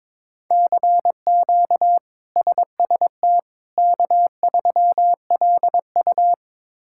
Morse code, heard as a keyed tone, about 21 wpm, 700 Hz. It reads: CQ SST K3LU
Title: K1USN SST QSO Practice - 22wpm